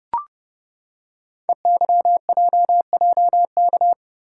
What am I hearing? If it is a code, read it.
Morse code, heard as a keyed tone, about 29 wpm, 700 Hz. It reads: EYJJK
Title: Code Groups - 5 Letters - 30wpm